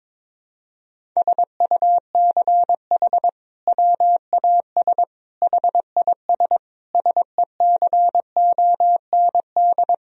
Morse code, heard as a keyed tone, about 21 wpm, 700 Hz. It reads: SUCH WAS HIS SECOND
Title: Sentences from Top 1000 Words - Review Part 1 - 22wpm